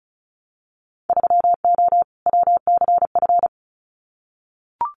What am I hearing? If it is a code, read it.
Morse code, heard as a keyed tone, about 36 wpm, 700 Hz. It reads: 3O WCF
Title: Winter Field Day Exchanges Encore 35wpm